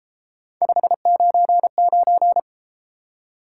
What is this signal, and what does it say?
Morse code, heard as a keyed tone, about 33 wpm, 700 Hz. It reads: 599